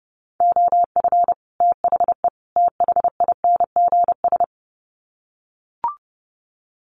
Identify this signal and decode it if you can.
Morse code, heard as a keyed tone, about 30 wpm, 700 Hz. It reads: OF THE THINGS
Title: Sets of 3 Words 30wpm